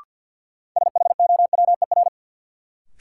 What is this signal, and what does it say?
Morse code, heard as a keyed tone, about 48 wpm, 700 Hz. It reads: SHOWER
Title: Sentences from Top 1600 Words - New Word Drill Alt - 50wpm